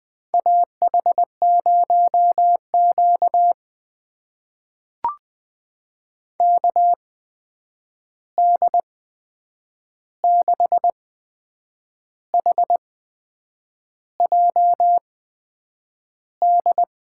Morse code, heard as a keyed tone, about 21 wpm, spaced as for 8 wpm, 700 Hz. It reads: AH0Q KD6HJD
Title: USA Call Signs - 8x Character Spacing - 20wpm